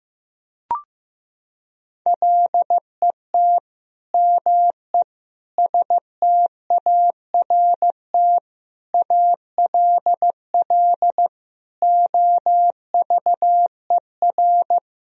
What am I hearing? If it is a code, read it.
Morse code, heard as a keyed tone, about 15 wpm, 700 Hz. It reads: LET ME START ALL OVER
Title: Sentences from Top 1000 Words - Review Part 1 - 15wpm